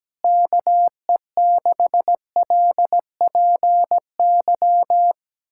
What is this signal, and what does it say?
Morse code, heard as a keyed tone, about 17 wpm, 700 Hz. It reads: KE6LPY
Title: USA Callsign DE Callsign - 6 Characters - 17wpm